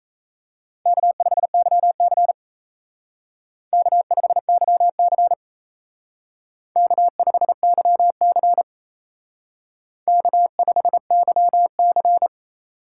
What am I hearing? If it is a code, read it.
Morse code, heard as a keyed tone, about 34 wpm, 700 Hz. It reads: K5YC K5YC K5YC K5YC
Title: Speed Racing - USA 1x2 Call Signs 28wpm